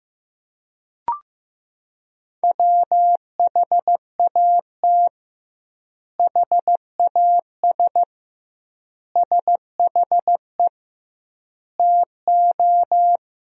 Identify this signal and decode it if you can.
Morse code, heard as a keyed tone, about 15 wpm, 700 Hz. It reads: WHAT HAS SHE TO
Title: 2x Word-Spacing - Sentences from Top 100 Words - 15wpm